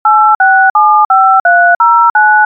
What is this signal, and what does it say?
DTMF, heard as telephone touch tones, tones of about 330 ms. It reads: 8675309
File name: Malexmedia Lazymanphone Dtmf 31799